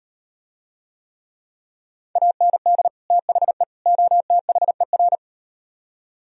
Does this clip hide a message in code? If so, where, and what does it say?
Yes: Morse code from 2.2 to 5.2 s, heard as a keyed tone, about 38 wpm, 700 Hz. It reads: AND THE OTHER